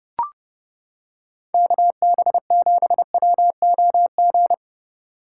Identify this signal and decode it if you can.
Morse code, heard as a keyed tone, about 29 wpm, 700 Hz. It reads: KB7WOG